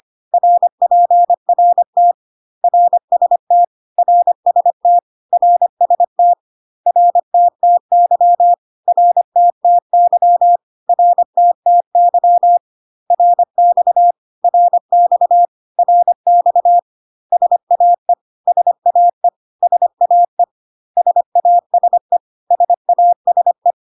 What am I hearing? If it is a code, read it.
Morse code, heard as a keyed tone, about 25 wpm, 700 Hz. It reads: RPRT RST RST RST RTTY RTTY RTTY RX RX RX SAE SAE SAE SASE SASE
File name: CW abbreviations alphabetical repeated 3 times 25 25wpm